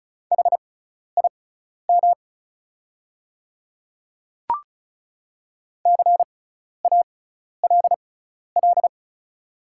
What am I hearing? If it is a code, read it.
Morse code, heard as a keyed tone, about 36 wpm, spaced as for 12 wpm, 690 Hz. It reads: HIM CALL
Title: 6x Character Spacing - Top 100 Words - 35wpm